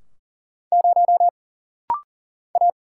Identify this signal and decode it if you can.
Morse code, heard as a keyed tone, about 40 wpm, 700 Hz. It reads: 0 A
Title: Single Letter-Number - Rapid-Fire 40wpm